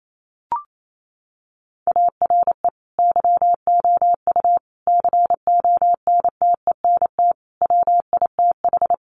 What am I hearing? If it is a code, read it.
Morse code, heard as a keyed tone, about 28 wpm, 690 Hz. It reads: ARE YOU CONTENT WITH
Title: Sentences from Top 900 Words 28wpm